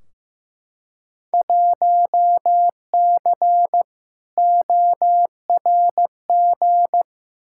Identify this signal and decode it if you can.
Morse code, heard as a keyed tone, about 15 wpm, 700 Hz. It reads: 1C ORG
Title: Field Day Exchanges - 15wpm